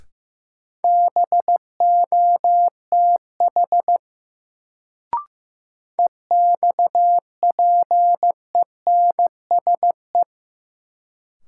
Morse code, heard as a keyed tone, about 15 wpm, 700 Hz. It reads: BOTH EXPENSE
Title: Sets of 4 Words - New Word Drill - 15wpm